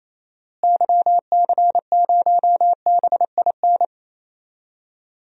Morse code, heard as a keyed tone, about 28 wpm, 700 Hz. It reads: YC0BIN
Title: DXCC-100 Call Signs - 6 Characters - Repeated 3x - 28wpm